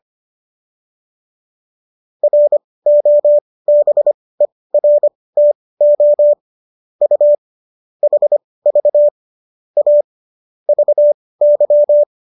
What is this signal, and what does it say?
Morse code, heard as a keyed tone, about 25 wpm, spaced as for 18 wpm, 590 Hz. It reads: ROBERTO U HV A VY